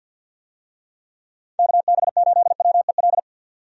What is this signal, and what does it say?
Morse code, heard as a keyed tone, about 51 wpm, 700 Hz. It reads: KB8WEL
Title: USA Call Signs - 6 Characters - 50wpm